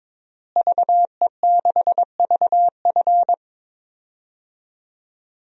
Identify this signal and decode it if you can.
Morse code, heard as a keyed tone, about 22 wpm, 700 Hz. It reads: VE6VF
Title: DXCC-100 Call Signs - 5 Characters - Repeated 2x - 22wpm